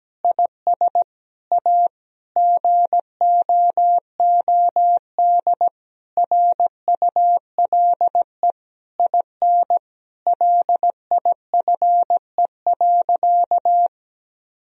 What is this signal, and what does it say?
Morse code, heard as a keyed tone, about 17 wpm, 700 Hz. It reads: IS A GOOD RULE IN LIFE.
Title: Sentences from Top 1000 Words - Review Part 2 - 17wpm